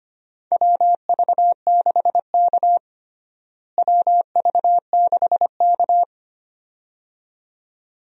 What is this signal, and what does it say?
Morse code, heard as a keyed tone, about 25 wpm, 700 Hz. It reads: WV6K WV6K
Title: USA Call Signs - 4 Characters - Repeated 2x - 25wpm